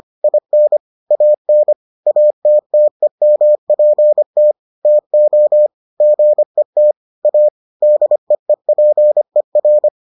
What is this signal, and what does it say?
Morse code, heard as a keyed tone, about 25 wpm, 600 Hz. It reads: IN AN ATTEMPT TO GET A DEEPER